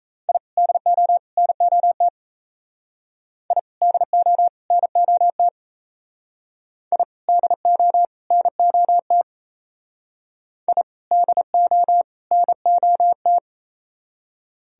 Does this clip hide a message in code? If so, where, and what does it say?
Yes: Morse code from 0.3 to 13.4 s, heard as a keyed tone, about 37 wpm, 700 Hz. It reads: I DO NOT I DO NOT I DO NOT I DO NOT